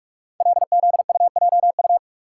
Morse code, heard as a keyed tone, about 44 wpm, 690 Hz. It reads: LZUJU
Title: Code Groups - 5 Letters - 45wpm